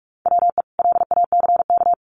Morse code, heard as a keyed tone, about 44 wpm, 700 Hz. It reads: WE LACK
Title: Sentences from Top 1400 Words 45wpm